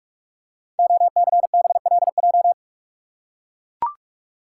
Morse code, heard as a keyed tone, about 44 wpm, 690 Hz. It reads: OCBLJ
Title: Code Groups - 5 Letters - 45wpm